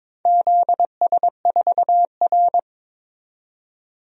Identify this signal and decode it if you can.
Morse code, heard as a keyed tone, about 22 wpm, 700 Hz. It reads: ZS4R
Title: DXCC-100 Call Signs - 4 Characters - 22wpm